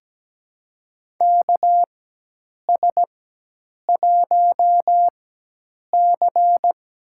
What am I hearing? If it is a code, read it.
Morse code, heard as a keyed tone, about 17 wpm, spaced as for 8 wpm, 700 Hz. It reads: KS1C
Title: USA Call Signs - 4 Characters - 4x Character Spacing - 17wpm